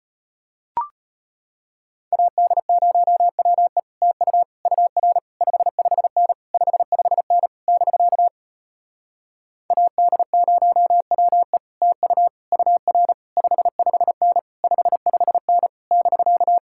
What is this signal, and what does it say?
Morse code, heard as a keyed tone, about 33 wpm, 700 Hz. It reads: AD0WE TU UR 55N 55N <BK> AD0WE TU UR 55N 55N <BK>